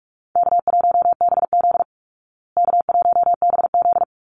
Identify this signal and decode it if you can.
Morse code, heard as a keyed tone, about 44 wpm, 700 Hz. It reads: K1BZ K1BZ
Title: Easy Call Signs Repeated 3x - 45wpm